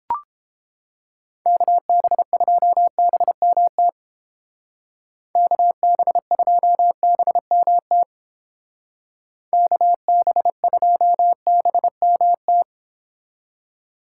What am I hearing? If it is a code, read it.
Morse code, heard as a keyed tone, about 28 wpm, 700 Hz. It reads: KB2BMT KB2BMT KB2BMT